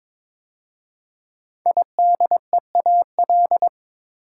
Morse code, heard as a keyed tone, about 23 wpm, 700 Hz. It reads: IDEAL
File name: 5 Letter Words - First 3 Letters Given - 22wpm